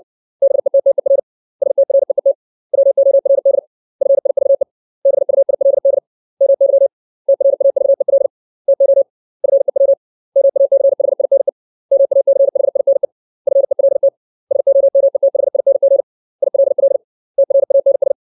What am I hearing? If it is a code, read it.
Morse code, heard as a keyed tone, about 60 wpm, 550 Hz. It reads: BETTER STREET WORD FIVE BUILD MY TRAVEL TO FEW MACHINE MACHINE FELT SOMETHING ILL TRANS
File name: 632 words / 632 Words 60WPM